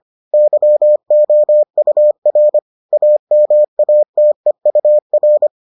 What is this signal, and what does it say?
Morse code, heard as a keyed tone, about 25 wpm, 600 Hz. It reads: YOUR AMATEUR